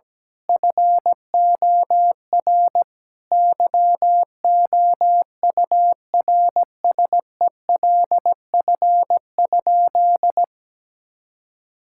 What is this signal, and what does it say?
Morse code, heard as a keyed tone, about 17 wpm, 700 Hz. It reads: FOR YOURSELF?
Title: Sentences from Top 1000 Words - Review Part 6 - 17wpm